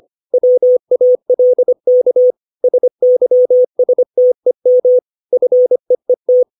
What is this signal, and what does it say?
Morse code, heard as a keyed tone, about 25 wpm, 500 Hz. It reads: WALK SYSTEM FEET
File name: 632 words / 632 Words 25WPM